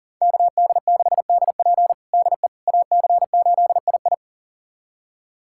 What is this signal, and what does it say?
Morse code, heard as a keyed tone, about 40 wpm, 700 Hz. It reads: KD6DP DE AC8II
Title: USA Callsign DE Callsign - 5 Characters - 40wpm